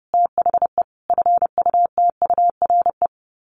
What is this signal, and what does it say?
Morse code, heard as a keyed tone, about 30 wpm, 700 Hz. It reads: THE FUTURE
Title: Sets of 3 Words 30wpm